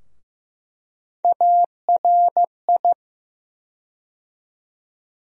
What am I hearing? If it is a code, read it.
Morse code, heard as a keyed tone, about 15 wpm, 700 Hz. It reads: ARI
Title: USA Call Sign Postfixes - 3 Letters 15wpm